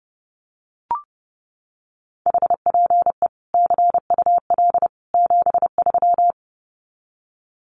Morse code, heard as a keyed tone, about 30 wpm, 700 Hz. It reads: HPE CUL 73
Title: Common QSO Phrases 30wpm